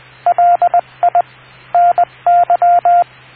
Morse code, heard as a keyed tone, about 20 wpm, 700 Hz, with radio noise. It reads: LI NY